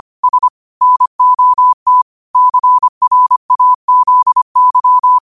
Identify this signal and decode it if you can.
Morse code, heard as a keyed tone, about 25 wpm, 1000 Hz. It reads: S NOT CRAZY